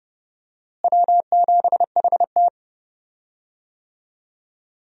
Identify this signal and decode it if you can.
Morse code, heard as a keyed tone, about 30 wpm, 700 Hz. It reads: W7HT
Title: USA Call Signs - 4 Characters - Repeated 3x - 30wpm